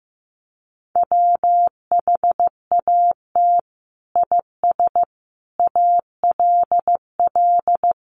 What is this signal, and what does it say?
Morse code, heard as a keyed tone, about 15 wpm, 700 Hz. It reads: WHAT IS ALL
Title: Sentences from Top 100 Words 15wpm